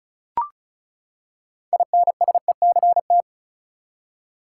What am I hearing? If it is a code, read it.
Morse code, heard as a keyed tone, about 36 wpm, 700 Hz. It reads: INSECT
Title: Six Letter Words - 35wpm